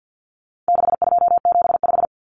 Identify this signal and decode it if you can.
Morse code, heard as a keyed tone, about 50 wpm, 700 Hz. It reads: 6275
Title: Common Cut Numbers - 4-Digit 50wpm